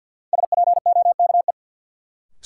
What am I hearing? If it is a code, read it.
Morse code, heard as a keyed tone, about 53 wpm, 700 Hz. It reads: SPOKE
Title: Sentences from Top 1200 Words - New Word Drill - 50wpm